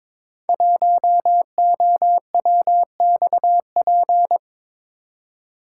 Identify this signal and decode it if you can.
Morse code, heard as a keyed tone, about 22 wpm, 700 Hz. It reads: 1OWXP
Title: Code Groups - 5 Letter:Numbers - 22wpm